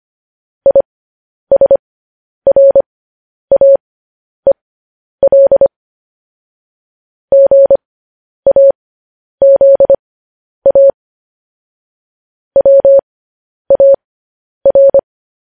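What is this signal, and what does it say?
Morse code, heard as a keyed tone, about 25 wpm, spaced as for 10 wpm, 550 Hz. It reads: ISRAEL GAZA WAR
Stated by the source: Morse code transmission of news headlines.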